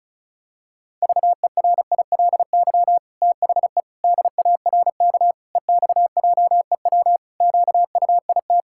Morse code, heard as a keyed tone, about 36 wpm, 700 Hz. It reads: VERILY THE DARK EXJEW QUIT
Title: Pangram Challenge v2 - 35wpm